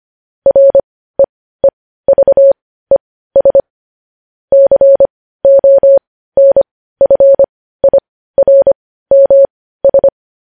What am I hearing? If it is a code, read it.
Morse code, heard as a keyed tone, about 25 wpm, spaced as for 15 wpm, 550 Hz. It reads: REEVES CONFIRMS
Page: news-headlines-in-morse-code-at-15-wpm